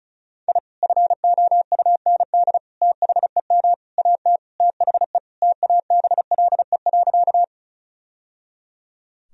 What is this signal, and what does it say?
Morse code, heard as a keyed tone, about 34 wpm, 700 Hz. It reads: I FOUND THEM AT THE TABLE.
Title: Sentences from Top 1000 Words - Review Part 2 - 35wpm